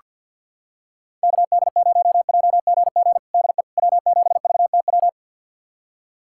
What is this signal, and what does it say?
Morse code, heard as a keyed tone, about 48 wpm, 700 Hz. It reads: KD0JGG DE W7VTW
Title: USA Callsign DE Callsign 50wpm